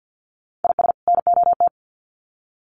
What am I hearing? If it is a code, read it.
Morse code, heard as a keyed tone, about 50 wpm, 700 Hz. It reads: IS NOT
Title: Sets of 2 Words 50wpm